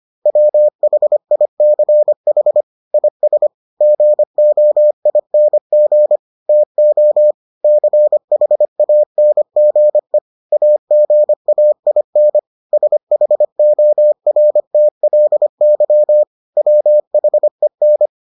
Morse code, heard as a keyed tone, about 25 wpm, 600 Hz. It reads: WHICH IS GOING TO CHANGE AGAIN SHORTLY WHEN